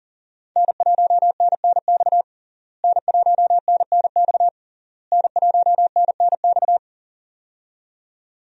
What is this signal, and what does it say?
Morse code, heard as a keyed tone, about 40 wpm, 700 Hz. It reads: N1NNX N1NNX N1NNX
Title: USA Call Signs - 5 Characters - Repeated 3x - 40wpm